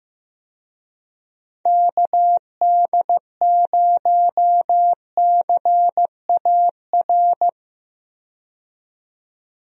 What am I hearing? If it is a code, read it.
Morse code, heard as a keyed tone, about 15 wpm, 700 Hz. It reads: KD0CAR
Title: USA Call Signs - 6 Characters - 15wpm